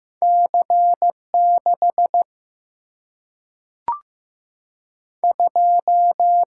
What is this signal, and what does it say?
Morse code, heard as a keyed tone, about 15 wpm, 700 Hz. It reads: C 6 2
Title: Single-Letter Single-Number 15wpm